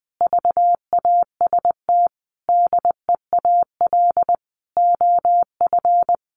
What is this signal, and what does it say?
Morse code, heard as a keyed tone, about 19 wpm, 700 Hz. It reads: VAST DEAL OF
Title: Sentences from Top 1300 Words 20wpm